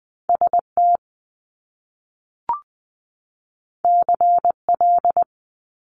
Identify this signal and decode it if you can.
Morse code, heard as a keyed tone, about 20 wpm, 700 Hz. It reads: S T C L
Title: Top Two Letter Combinations in Words 20wpm